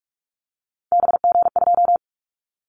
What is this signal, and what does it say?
Morse code, heard as a keyed tone, about 46 wpm, 700 Hz. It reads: BG2
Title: Callsign Prefixes for Top 100 DXCC - 45wpm